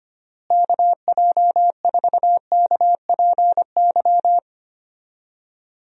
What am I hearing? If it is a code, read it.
Morse code, heard as a keyed tone, about 25 wpm, 700 Hz. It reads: KJ4KPY
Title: USA Call Signs - 6 Characters - Repeated 2x - 25wpm